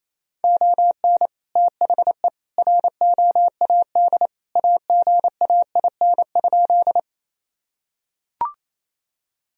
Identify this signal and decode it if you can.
Morse code, heard as a keyed tone, about 28 wpm, 700 Hz. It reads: ON THE ROAD AGAIN?
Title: Sentences from Top 1000 Words - Review Part 2 - 28wpm